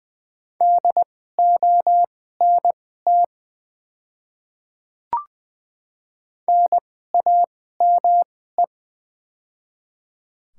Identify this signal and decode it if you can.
Morse code, heard as a keyed tone, about 20 wpm, spaced as for 14 wpm, 700 Hz. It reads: DONT NAME
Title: 2x Character Spacing - Top 300 Words - 20wpm